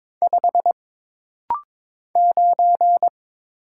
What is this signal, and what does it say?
Morse code, heard as a keyed tone, about 22 wpm, 700 Hz. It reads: 5 9
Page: This rapid-fire timing is geared to push your ICR (Instant Character Recognition) skills.